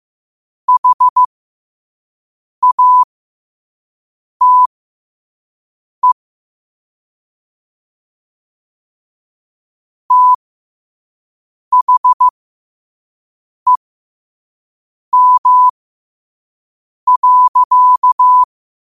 Morse code, heard as a keyed tone, about 15 wpm, spaced as for 5 wpm, 1000 Hz. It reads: HATE THEM.
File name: qotdfiles / 5wpm